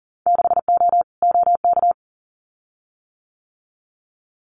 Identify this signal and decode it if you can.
Morse code, heard as a keyed tone, about 40 wpm, 700 Hz. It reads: 6O OK
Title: Winter Field Day Exchanges Encore 40wpm